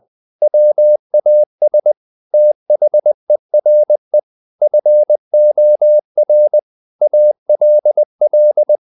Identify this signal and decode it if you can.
Morse code, heard as a keyed tone, about 20 wpm, 600 Hz. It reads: WAS THERE FOR ALL
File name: the old man and the sea / the old man and the sea part 1 20wpm